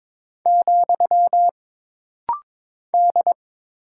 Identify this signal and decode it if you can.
Morse code, heard as a keyed tone, about 22 wpm, 700 Hz. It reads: , D
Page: This rapid-fire timing is geared to push your ICR (Instant Character Recognition) skills.